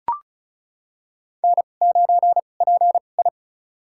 Morse code, heard as a keyed tone, about 36 wpm, spaced as for 26 wpm, 700 Hz. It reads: N9PI
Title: USA Call Signs - 4 Characters - 2x Character Spacing - 35wpm